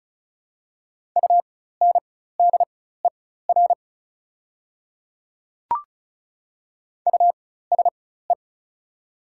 Morse code, heard as a keyed tone, about 34 wpm, spaced as for 16 wpm, 700 Hz. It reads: UNDER USE
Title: 4x Character Spacing - Top 300 Words - 35wpm